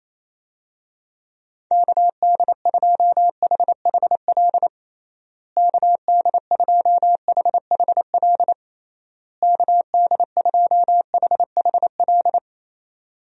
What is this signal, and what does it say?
Morse code, heard as a keyed tone, about 28 wpm, 700 Hz. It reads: KD2HHL KD2HHL KD2HHL
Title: USA Call Signs - 6 Characters - Repeated 3x - 28wpm